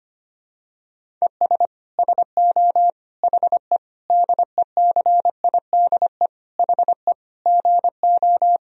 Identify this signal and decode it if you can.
Morse code, heard as a keyed tone, about 25 wpm, 700 Hz. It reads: ES SO HE DECIDE HE GO
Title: Sentences from Top 500 Words and Abbreviations 25wpm